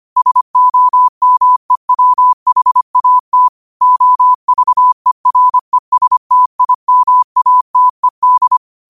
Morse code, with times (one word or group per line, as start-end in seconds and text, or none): 0.2-3.5 SOMEWHAT
3.8-8.6 OVERESTIMATED